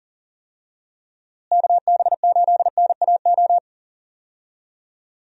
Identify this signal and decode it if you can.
Morse code, heard as a keyed tone, about 40 wpm, 700 Hz. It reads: KB8NAO
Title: USA Call Signs - 6 Characters - Repeated 2x - 40wpm